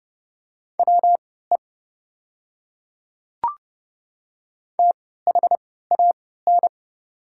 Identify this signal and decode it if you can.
Morse code, heard as a keyed tone, about 29 wpm, spaced as for 17 wpm, 690 Hz. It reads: WE THAN
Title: 3x Character Spacing - Top 100 Words - 30wpm